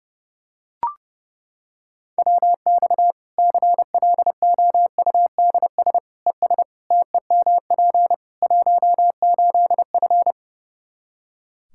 Morse code, heard as a keyed tone, about 30 wpm, 700 Hz. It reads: WX CLOUDS ES TEMP 18F
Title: Practice QSO - Exchange 2 - Info - 30wpm